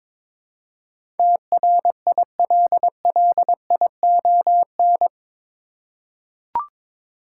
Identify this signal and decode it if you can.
Morse code, heard as a keyed tone, about 22 wpm, 700 Hz. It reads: TRILLION